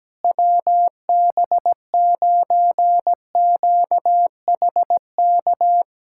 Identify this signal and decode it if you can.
Morse code, heard as a keyed tone, about 17 wpm, 700 Hz. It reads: WB9QHK
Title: USA Callsign DE Callsign 17wpm